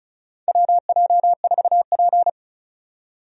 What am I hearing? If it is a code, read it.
Morse code, heard as a keyed tone, about 36 wpm, 700 Hz. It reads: WJ4P
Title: USA Call Signs - 4 Characters - 35wpm